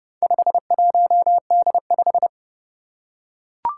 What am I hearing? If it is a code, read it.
Morse code, heard as a keyed tone, about 30 wpm, 700 Hz. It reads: 51D5
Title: Four Challenging Letter-Number Combinations 30wpm